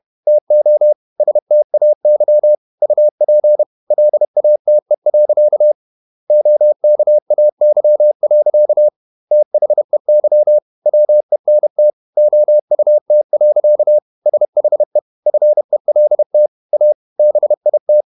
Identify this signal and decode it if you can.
Morse code, heard as a keyed tone, about 31 wpm, 600 Hz. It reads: TO STAY UP LATE. OKAY. THEY WENT OUT. SHE FELT A BIT
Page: Never 31wpm | CW med Gnister